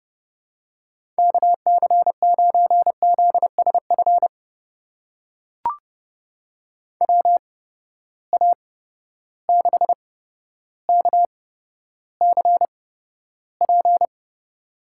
Morse code, heard as a keyed tone, about 31 wpm, spaced as for 13 wpm, 700 Hz. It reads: KC9ZSF WA6KCP
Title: USA Call Signs - 6 Characters - 8x Character Spacing - 30wpm